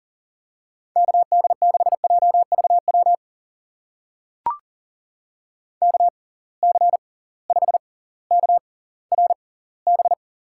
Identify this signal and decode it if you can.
Morse code, heard as a keyed tone, about 42 wpm, spaced as for 21 wpm, 700 Hz. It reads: KD6JVW KC5KRB
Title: USA Call Signs - 6 Characters - 6x Character Spacing - 40wpm